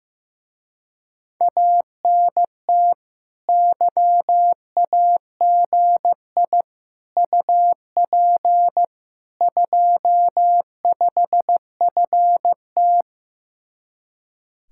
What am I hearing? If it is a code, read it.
Morse code, heard as a keyed tone, about 16 wpm, 700 Hz. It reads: ANT YAGI UP 25FT
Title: Speed-Racing Common QSO Phrases Encore - 15wpm